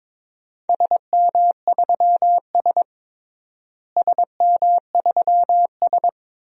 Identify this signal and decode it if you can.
Morse code, heard as a keyed tone, about 22 wpm, 700 Hz. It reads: SM3S SM3S
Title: DXCC-100 Call Signs - 4 Characters - Repeated 3x - 22wpm